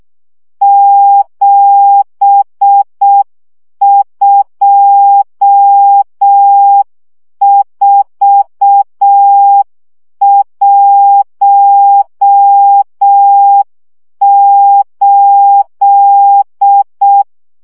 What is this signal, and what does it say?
Morse code, heard as a keyed tone, about 6 wpm, 800 Hz. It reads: 72418
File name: morze vizsga 2011-1 5